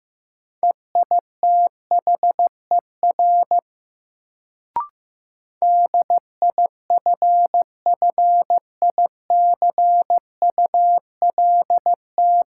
Morse code, heard as a keyed tone, about 16 wpm, 700 Hz. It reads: EITHER DIFFICULT
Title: Sentences from Top 600 Words - New Word Drill - 15wpm